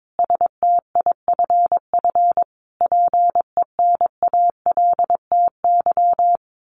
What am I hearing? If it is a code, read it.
Morse code, heard as a keyed tone, about 22 wpm, 700 Hz. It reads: STIFF PENALTY
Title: A Sound Of Thunder 22wpm